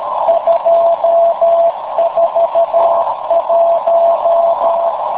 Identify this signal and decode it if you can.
Morse code, heard as a keyed tone, about 13 wpm, 670 Hz, with noise. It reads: 241
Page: In the afternoon at 1130z TR0A/BCN